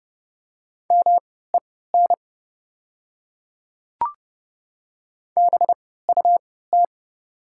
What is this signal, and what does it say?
Morse code, heard as a keyed tone, about 30 wpm, spaced as for 17 wpm, 700 Hz. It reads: MEN BUT
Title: 3x Character Spacing - Top 200 Words - 30wpm